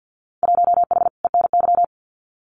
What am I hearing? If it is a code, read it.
Morse code, heard as a keyed tone, about 48 wpm, 700 Hz. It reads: 1H ENY
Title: Winter Field Day Exchanges 50wpm